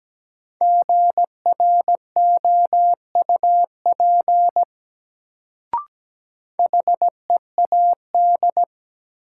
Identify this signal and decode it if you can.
Morse code, heard as a keyed tone, about 17 wpm, 700 Hz. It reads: GROUP HEAD
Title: Sets of 3 Words - New Word Drill - 17wpm